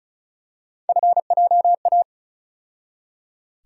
Morse code, heard as a keyed tone, about 34 wpm, 700 Hz. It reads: FJA
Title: USA Call Sign Postfixes - 3 Letters 35wpm